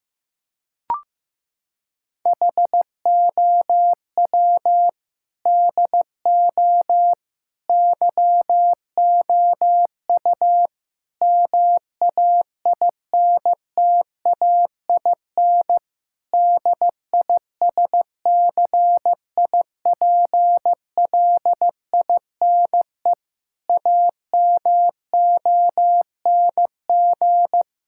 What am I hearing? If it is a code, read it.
Morse code, heard as a keyed tone, about 15 wpm, 700 Hz. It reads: HOW DO YOU MAINTAIN DISCIPLINE AMONG
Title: Sentences from Top 1000 Words - Review Part 6 - 15wpm